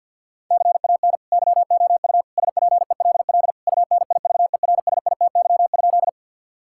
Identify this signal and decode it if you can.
Morse code, heard as a keyed tone, about 50 wpm, 700 Hz. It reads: CAN YOU SPELL UNIVERSITY?
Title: Sentences from Top 1000 Words - Review Part 5 - 50wpm